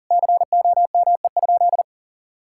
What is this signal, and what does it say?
Morse code, heard as a keyed tone, about 40 wpm, 700 Hz. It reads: COME?
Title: Sentences from Top 1600 Words 40wpm